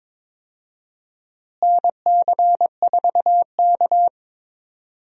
Morse code, heard as a keyed tone, about 22 wpm, 700 Hz. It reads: NC4K
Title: USA Call Signs - 4 Characters - Repeated 2x - 22wpm